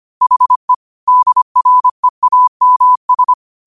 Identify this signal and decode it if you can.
Morse code, heard as a keyed tone, about 25 wpm, 1000 Hz. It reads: HE DREAMS